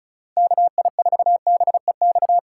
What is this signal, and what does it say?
Morse code, heard as a keyed tone, about 36 wpm, 700 Hz. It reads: KI4BEX
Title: USA Call Signs - 6x Character Spacing - 35wpm